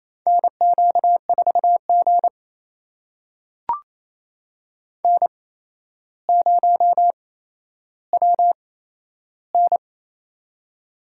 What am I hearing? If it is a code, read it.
Morse code, heard as a keyed tone, about 28 wpm, spaced as for 12 wpm, 700 Hz. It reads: NQ4G N0WN
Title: USA Call Signs - 4 Characters - 8x Character Spacing - 28wpm